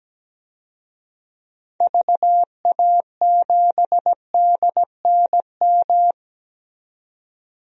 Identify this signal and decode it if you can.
Morse code, heard as a keyed tone, about 17 wpm, 700 Hz. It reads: VA7DNM
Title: DXCC-100 Call Signs - 6 Characters - 17wpm